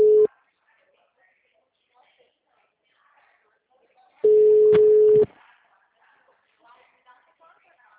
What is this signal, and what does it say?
Ringback tone (Europe, 425 Hz), with speech.